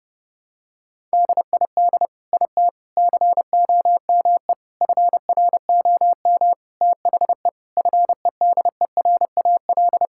Morse code, heard as a keyed tone, about 29 wpm, 700 Hz. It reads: DID IT COME FROM THE FEDERAL
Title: Sentences from Top 1000 Words - Review Part 5 - 30wpm